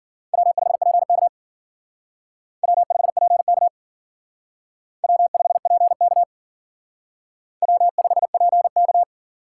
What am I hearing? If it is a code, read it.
Morse code, heard as a keyed tone, about 51 wpm, 700 Hz. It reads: W5PK W5PK W5PK W5PK